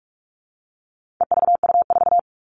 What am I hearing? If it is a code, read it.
Morse code, heard as a keyed tone, about 44 wpm, 700 Hz. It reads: EVU4
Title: All Cut Numbers - 4-Digit 45wpm